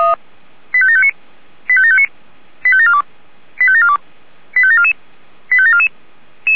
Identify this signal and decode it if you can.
Five-tone selective call (ZVEI), heard as quick runs of five tones, tones of about 70 ms. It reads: 85759 85759 85742 85742 85744 85744